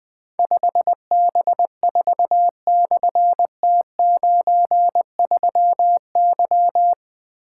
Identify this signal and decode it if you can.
Morse code, heard as a keyed tone, about 20 wpm, 700 Hz. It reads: 5B4/T93Y
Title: Difficult Call Signs 20wpm